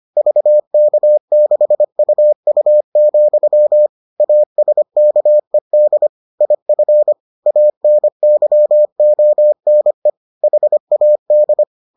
Morse code, heard as a keyed tone, about 25 wpm, 600 Hz. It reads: VK6UU, ASKED IF ANYONE HAD